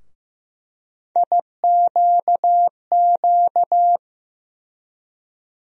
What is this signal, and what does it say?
Morse code, heard as a keyed tone, about 15 wpm, 700 Hz. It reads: IQQ
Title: USA Call Sign Postfixes - 3 Letters 15wpm